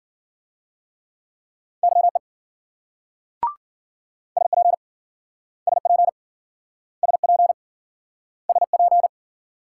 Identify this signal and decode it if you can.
Morse code, heard as a keyed tone, about 53 wpm, 700 Hz. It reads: YE SP SP SP SP